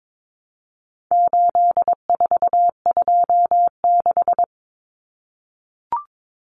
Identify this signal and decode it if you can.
Morse code, heard as a keyed tone, about 22 wpm, 700 Hz. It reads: 8426
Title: Numbers - 4-Digit 22wpm